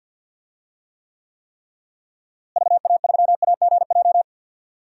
Morse code, heard as a keyed tone, about 52 wpm, 700 Hz. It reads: VA3AGJ
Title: DXCC-100 Call Signs - 6 Characters - Repeated 3x - 50wpm